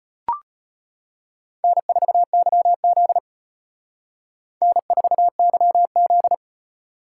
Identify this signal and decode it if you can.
Morse code, heard as a keyed tone, about 32 wpm, 700 Hz. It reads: N4YZ N4YZ